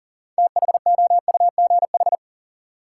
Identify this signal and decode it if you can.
Morse code, heard as a keyed tone, about 40 wpm, 700 Hz. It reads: THOUGH
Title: Sentences from Top 400 Words - New Word Drill - 40wpm